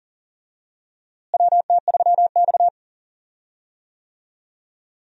Morse code, heard as a keyed tone, about 40 wpm, 700 Hz. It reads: WT3X
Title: USA Call Signs - 4 Characters - Repeated 2x - 40wpm